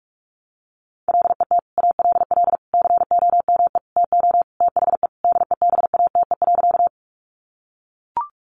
Morse code, heard as a keyed tone, about 44 wpm, 700 Hz. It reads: LET ALL COME TO THE DEBATE.
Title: Sentences from Top 1100 Words 45wpm